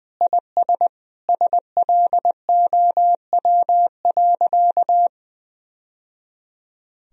Morse code, heard as a keyed tone, about 21 wpm, 700 Hz. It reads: IS SLOW.
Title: Sentences from Top 1000 Words - Review Part 2 - 20wpm